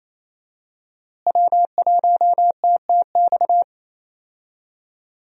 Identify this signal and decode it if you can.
Morse code, heard as a keyed tone, about 28 wpm, 700 Hz. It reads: W1TTX
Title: USA Call Signs - 5 Characters - Repeated 2x - 28wpm